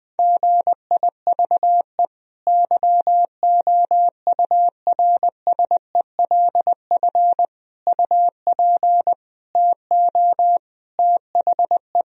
Morse code, heard as a keyed tone, about 20 wpm, 700 Hz. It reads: GIVE YOURSELF UP TO THE
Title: Sentences from Top 1000 Words - Review Part 4 - 20wpm